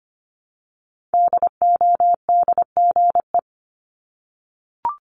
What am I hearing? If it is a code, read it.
Morse code, heard as a keyed tone, about 25 wpm, 700 Hz. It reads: DODGE
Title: Multi-letter Phonemes 25wpm